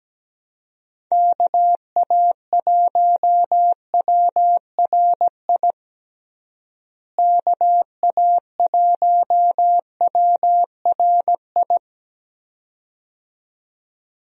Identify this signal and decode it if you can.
Morse code, heard as a keyed tone, about 17 wpm, 700 Hz. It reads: KA1WRI KA1WRI
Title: USA Call Signs - 6 Characters - Repeated 2x - 17wpm